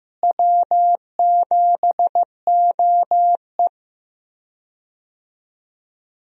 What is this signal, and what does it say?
Morse code, heard as a keyed tone, about 16 wpm, 700 Hz. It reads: W7OE
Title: USA Call Signs - 4 Characters - Repeated 3x - 15wpm